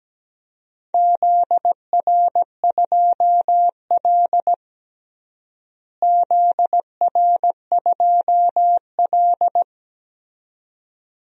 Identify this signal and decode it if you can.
Morse code, heard as a keyed tone, about 17 wpm, 700 Hz. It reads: ZR2L ZR2L
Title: DXCC-100 Call Signs - 4 Characters - Repeated 2x - 17wpm